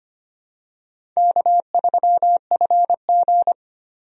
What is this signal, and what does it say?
Morse code, heard as a keyed tone, about 25 wpm, 700 Hz. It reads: K3FG
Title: USA Call Signs - 4 Characters - Repeated 2x - 25wpm